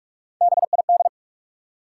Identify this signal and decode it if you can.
Morse code, heard as a keyed tone, about 47 wpm, 700 Hz. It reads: BID
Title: Three Letter Words - 45wpm